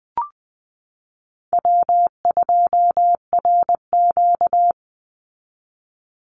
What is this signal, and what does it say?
Morse code, heard as a keyed tone, about 20 wpm, 700 Hz. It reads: W2RQ
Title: Call Signs Repeated 2x - 20wpm